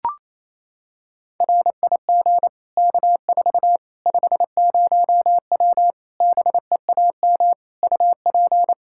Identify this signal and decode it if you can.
Morse code, heard as a keyed tone, about 28 wpm, 700 Hz. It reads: RIG K4 50W BEAM UP
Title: Practice QSO - Exchange 2 - Info - 28wpm